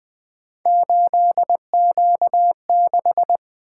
Morse code, heard as a keyed tone, about 20 wpm, 700 Hz. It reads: 8Q6
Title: Three Letter-Number Combinations - 20wpm